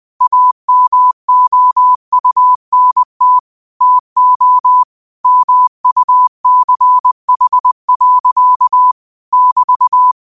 Morse code, heard as a keyed tone, about 20 wpm, 1000 Hz. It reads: AMOUNT TO MUCH. <BT>
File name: 20wpm